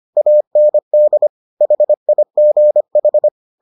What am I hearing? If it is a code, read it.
Morse code, heard as a keyed tone, about 25 wpm, 600 Hz. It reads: AND HIGH